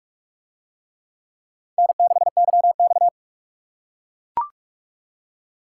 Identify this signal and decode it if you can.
Morse code, heard as a keyed tone, about 44 wpm, 700 Hz. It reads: N6YX